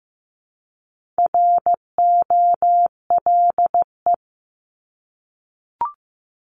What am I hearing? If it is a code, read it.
Morse code, heard as a keyed tone, about 15 wpm, 700 Hz. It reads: ROLE
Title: Top 751-1000 Words 15wpm